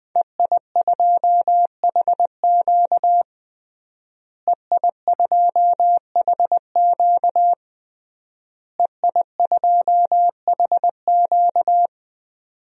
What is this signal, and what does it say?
Morse code, heard as a keyed tone, about 21 wpm, 700 Hz. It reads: EI2HQ EI2HQ EI2HQ
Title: DXCC-100 Call Signs - 5 Characters - Repeated 3x - 20wpm